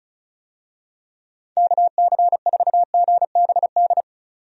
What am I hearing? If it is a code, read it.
Morse code, heard as a keyed tone, about 34 wpm, 700 Hz. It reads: KC4GBD
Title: USA Call Signs - 6 Characters - 35wpm